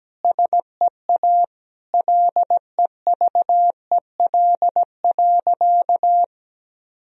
Morse code, heard as a keyed tone, about 17 wpm, 700 Hz. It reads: SEA LEVEL.
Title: Sentences from Top 1000 Words - Review Part 4 - 17wpm